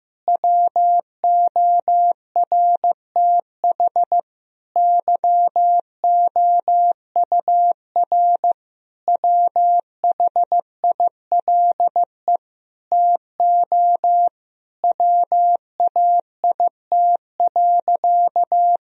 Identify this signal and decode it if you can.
Morse code, heard as a keyed tone, about 16 wpm, 700 Hz. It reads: WORTH YOUR WHILE TO WAIT.
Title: Sentences from Top 1000 Words - Review Part 1 - 15wpm